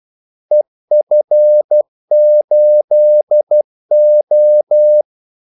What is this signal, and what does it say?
Morse code, heard as a keyed tone, about 12 wpm, 600 Hz. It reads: EF8O
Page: 2 ekstra ordmellemrum mellem kaldesignalerne